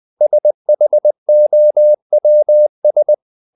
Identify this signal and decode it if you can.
Morse code, heard as a keyed tone, about 20 wpm, 600 Hz. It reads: SHOWS
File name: the old man and the sea / the old man and the sea part 1 20wpm